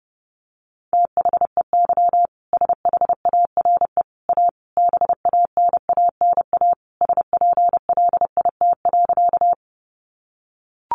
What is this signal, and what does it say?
Morse code, heard as a keyed tone, about 30 wpm, 700 Hz. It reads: THEY SHARE A BANANA SPLIT.
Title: Easy Phrases 30wpm